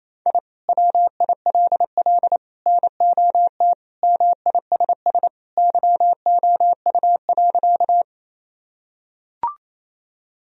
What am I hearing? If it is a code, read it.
Morse code, heard as a keyed tone, about 28 wpm, 700 Hz. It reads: I WILL NOT MISS YOU.
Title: Sentences from Top 1000 Words - Review Part 3 - 28wpm